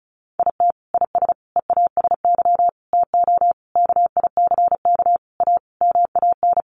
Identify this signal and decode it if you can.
Morse code, heard as a keyed tone, about 34 wpm, 700 Hz. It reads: IT IS EASY TO KICK A MAN
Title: Sentences from Top 1300 Words 35wpm